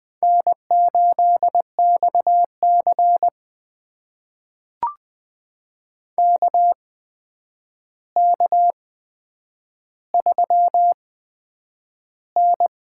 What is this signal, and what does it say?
Morse code, heard as a keyed tone, about 20 wpm, spaced as for 9 wpm, 700 Hz. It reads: N8XC KK3N
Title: USA Call Signs - 4 Characters - 8x Character Spacing - 20wpm